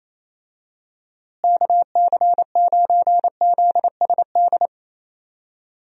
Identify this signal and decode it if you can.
Morse code, heard as a keyed tone, about 27 wpm, 700 Hz. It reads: KC9ZSD
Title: USA Call Signs - 6 Characters - Repeated 3x - 28wpm